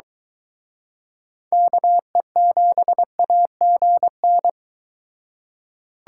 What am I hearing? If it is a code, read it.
Morse code, heard as a keyed tone, about 22 wpm, 700 Hz. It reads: KE7AGN